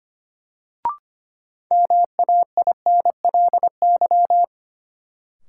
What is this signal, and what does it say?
Morse code, heard as a keyed tone, about 25 wpm, 700 Hz. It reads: MAINLY
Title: Sentences from Top 1400 Words - New Word Drill Alt - 25wpm